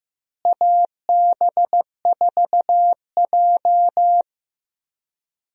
Morse code, heard as a keyed tone, about 15 wpm, 700 Hz. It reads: AB4J
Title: USA Call Signs - 4 Characters - Repeated 3x - 15wpm